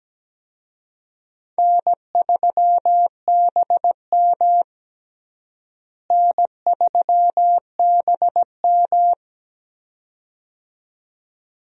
Morse code, heard as a keyed tone, about 17 wpm, 700 Hz. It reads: N3BM N3BM
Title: USA Call Signs - 4 Characters - Repeated 2x - 17wpm